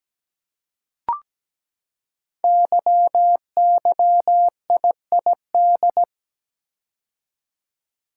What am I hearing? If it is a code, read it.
Morse code, heard as a keyed tone, about 17 wpm, 700 Hz. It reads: YYIID
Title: Code Groups - 5 Letter:Numbers - 17wpm